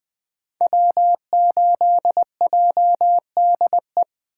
Morse code, heard as a keyed tone, about 20 wpm, 700 Hz. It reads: W8JDE
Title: 2023 POTA Activator Call Signs - 20wpm